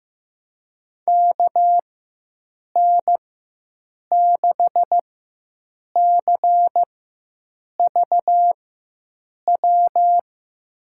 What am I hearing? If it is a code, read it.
Morse code, heard as a keyed tone, about 15 wpm, spaced as for 7 wpm, 700 Hz. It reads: KN6CVW
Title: USA Call Signs - 6 Characters - 4x Character Spacing - 15wpm